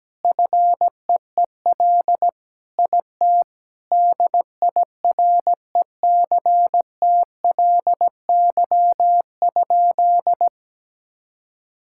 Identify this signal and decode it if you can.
Morse code, heard as a keyed tone, about 17 wpm, 700 Hz. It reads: FEEL IT DIRECTLY?
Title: Sentences from Top 1000 Words - Review Part 5 - 17wpm